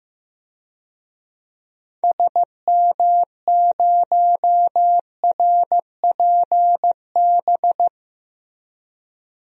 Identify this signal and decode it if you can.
Morse code, heard as a keyed tone, about 15 wpm, 700 Hz. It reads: SM0RPB
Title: DXCC-100 Call Signs - 6 Characters - Repeated 3x - 15wpm